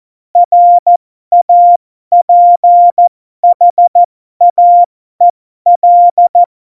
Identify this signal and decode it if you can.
Morse code, heard as a keyed tone, about 14 wpm, spaced as for 12 wpm, 700 Hz. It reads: RAPHAEL
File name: Morsekurs / Witze